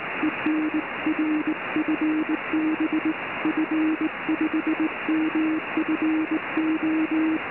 Morse code, heard as a keyed tone, about 20 wpm, 310 Hz, with noise. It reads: EN R FB F5MFO